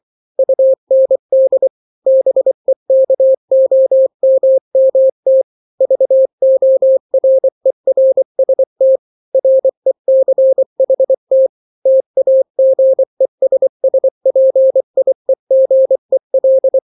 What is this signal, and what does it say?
Morse code, heard as a keyed tone, about 24 wpm, 530 Hz. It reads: UND BEKOMMT VORERST RECHT TAGESSPIEGEL
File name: daily / 23wpm